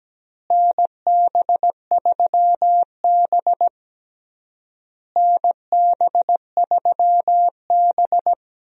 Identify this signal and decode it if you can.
Morse code, heard as a keyed tone, about 17 wpm, 700 Hz. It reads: NB3B NB3B
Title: USA Call Signs - 4 Characters - Repeated 3x - 17wpm